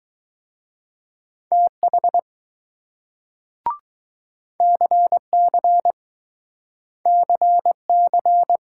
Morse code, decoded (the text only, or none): TH CC CC